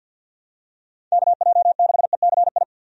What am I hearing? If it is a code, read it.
Morse code, heard as a keyed tone, about 48 wpm, 700 Hz. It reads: KJ6ECI
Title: USA Call Signs - 6 Characters - 50wpm